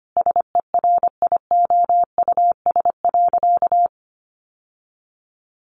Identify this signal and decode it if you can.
Morse code, heard as a keyed tone, about 25 wpm, 700 Hz. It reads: SERIOUS.
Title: Sentences from Top 1100 Words 25wpm